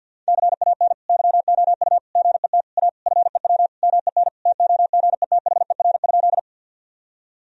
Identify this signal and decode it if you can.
Morse code, heard as a keyed tone, about 52 wpm, 700 Hz. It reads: CAN YOU GET A FEW MEN TOGETHER?
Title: Sentences from Top 1000 Words - Review Part 1 - 50wpm